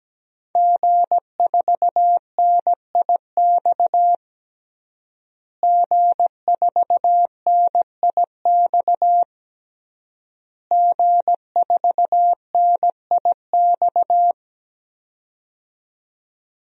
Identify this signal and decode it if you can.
Morse code, heard as a keyed tone, about 17 wpm, 700 Hz. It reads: G4NIX G4NIX G4NIX
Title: DXCC-100 Call Signs - 5 Characters - Repeated 3x - 17wpm